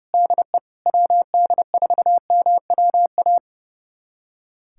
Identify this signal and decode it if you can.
Morse code, heard as a keyed tone, about 30 wpm, 700 Hz. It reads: DE WD4MWA
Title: USA Callsign DE Callsign 30wpm